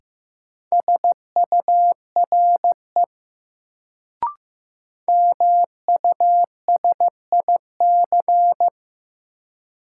Morse code, heard as a keyed tone, about 15 wpm, 700 Hz. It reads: SURE MUSIC
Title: Sentences from Top 300 Words - New Word Drill - 15wpm